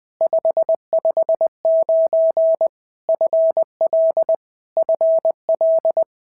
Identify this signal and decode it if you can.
Morse code, heard as a keyed tone, about 20 wpm, 650 Hz. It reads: 559 FL FL
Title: POTA QSO Practice - 20wpm